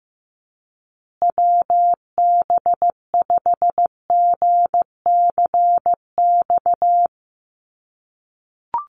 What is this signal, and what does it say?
Morse code, heard as a keyed tone, about 14 wpm, 700 Hz. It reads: WB5GCX
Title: USA Call Signs Repeated 2x 15wpm